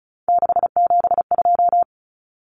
Morse code, heard as a keyed tone, about 34 wpm, 700 Hz. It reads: 672
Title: Numbers - 3-Digit 35wpm